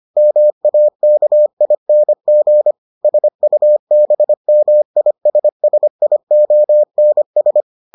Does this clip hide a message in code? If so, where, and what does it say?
Yes: Morse code from 0.2 to 7.6 s, heard as a keyed tone, about 25 wpm, 600 Hz. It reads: MAKING SUBMISSIONS